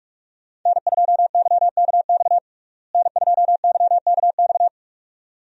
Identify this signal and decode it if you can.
Morse code, heard as a keyed tone, about 45 wpm, 700 Hz. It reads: N2YKX N2YKX
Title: USA Call Signs - 5 Characters - Repeated 2x - 45wpm